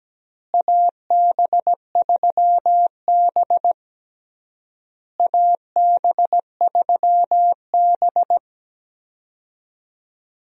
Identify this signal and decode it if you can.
Morse code, heard as a keyed tone, about 17 wpm, 700 Hz. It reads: AB3B AB3B
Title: USA Call Signs - 4 Characters - Repeated 3x - 17wpm